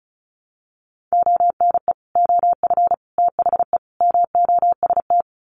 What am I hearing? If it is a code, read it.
Morse code, heard as a keyed tone, about 34 wpm, 700 Hz. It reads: ONE OF THE MOST
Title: Sets of 4 Words 35wpm